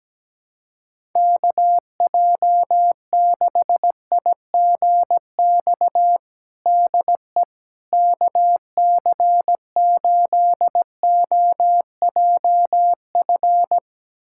Morse code, heard as a keyed tone, about 17 wpm, 700 Hz. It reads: KJ6IGX DE KC8OJF
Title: USA Callsign DE Callsign - 6 Characters - 17wpm